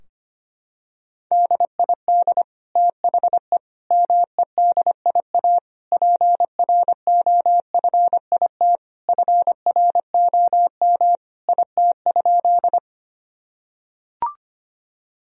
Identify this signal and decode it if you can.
Morse code, heard as a keyed tone, about 25 wpm, 700 Hz. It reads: DID THE MEDIA PROFIT FROM IT?
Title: Sentences from Top 1000 Words - Review Part 4 - 25wpm